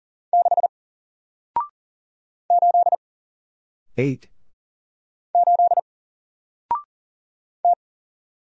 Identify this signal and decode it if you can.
Morse code, heard as a keyed tone, about 40 wpm, 700 Hz. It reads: 6 8 8 T
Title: Single Letter-Number Alt 40wpm